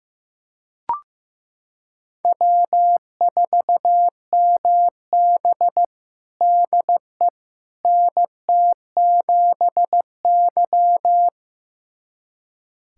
Morse code, heard as a keyed tone, about 16 wpm, 700 Hz. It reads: W4MB DE NT7Y
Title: USA Callsign DE Callsign - 4 Characters - 15wpm